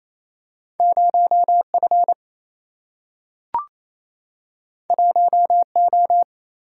Morse code, heard as a keyed tone, about 28 wpm, 700 Hz. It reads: 0 F 1 O
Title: Single-Letter Single-Number 28wpm